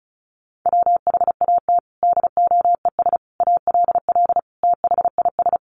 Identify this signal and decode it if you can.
Morse code, heard as a keyed tone, about 34 wpm, 700 Hz. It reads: WHAT DOES ALL THIS
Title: Sentences from Top 200 Words 35wpm